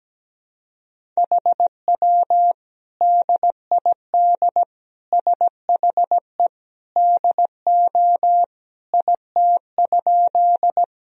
Morse code, heard as a keyed tone, about 17 wpm, 700 Hz. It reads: HW DID SHE DO IT?
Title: Sentences from Top 100 Words and Abbreviations 17wpm